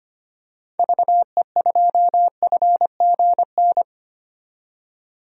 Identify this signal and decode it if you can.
Morse code, heard as a keyed tone, about 25 wpm, 690 Hz. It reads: VE2FGN
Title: DXCC-100 Call Signs - 6 Characters - 25wpm